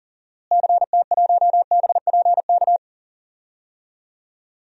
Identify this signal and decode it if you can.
Morse code, heard as a keyed tone, about 39 wpm, 700 Hz. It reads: CT1BPK
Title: DXCC-100 Call Signs - 6 Characters - Repeated 3x - 40wpm